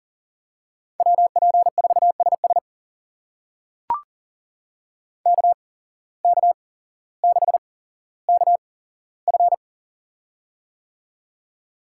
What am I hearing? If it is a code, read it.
Morse code, heard as a keyed tone, about 42 wpm, spaced as for 17 wpm, 700 Hz. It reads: WP4SS KK6KF
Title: USA Call Signs - 5 Characters - 8x Character Spacing - 40wpm